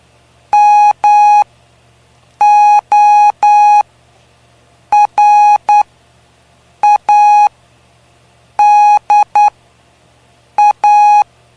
Morse code, heard as a keyed tone, about 9 wpm, spaced as for 6 wpm, 830 Hz. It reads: MORADA
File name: cw / py2lcd / 05-PY2LCD-Li��o 5